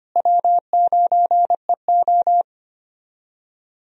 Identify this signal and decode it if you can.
Morse code, heard as a keyed tone, about 25 wpm, 700 Hz. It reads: W9EO
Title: USA Call Signs - 4 Characters - Repeated 3x - 25wpm